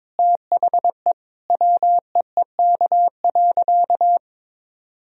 Morse code, heard as a keyed tone, about 22 wpm, 700 Hz. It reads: THE WEEK.
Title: Sentences from Top 1000 Words - Review Part 2 - 22wpm